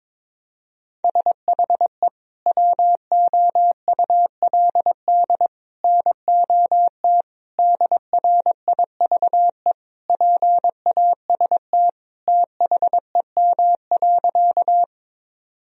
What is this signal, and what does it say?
Morse code, heard as a keyed tone, about 22 wpm, 700 Hz. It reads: SHE WOULD NOT DRIVE PAST THEM.
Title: Sentences from Top 1000 Words - Review Part 3 - 22wpm